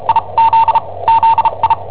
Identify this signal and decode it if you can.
Morse code, heard as a keyed tone, about 37 wpm, spaced as for 26 wpm, 910 Hz, with radio noise. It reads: IZZI
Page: Cassetta esercizi preparata da I7OHP per IW5ELP